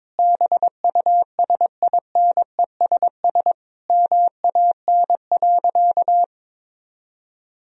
Morse code, heard as a keyed tone, about 21 wpm, 700 Hz. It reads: BUSINESS MAN.
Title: Sentences from Top 1000 Words - Review Part 4 - 22wpm